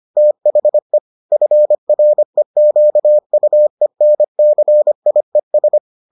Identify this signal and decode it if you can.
Morse code, heard as a keyed tone, about 25 wpm, 600 Hz. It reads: THE FREQUENCIES